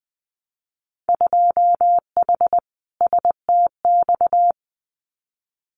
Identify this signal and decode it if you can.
Morse code, heard as a keyed tone, about 20 wpm, 700 Hz. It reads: 2H STX
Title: Winter Field Day Exchanges Encore 20wpm